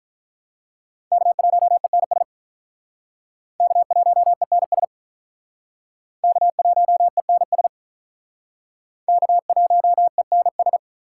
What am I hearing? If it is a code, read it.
Morse code, heard as a keyed tone, about 47 wpm, 700 Hz. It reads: K1ENS K1ENS K1ENS K1ENS